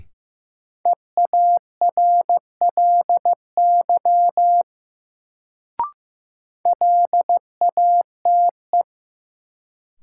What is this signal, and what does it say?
Morse code, heard as a keyed tone, about 15 wpm, 700 Hz. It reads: EARLY LATE
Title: Sets of 3 Words - New Word Drill - 15wpm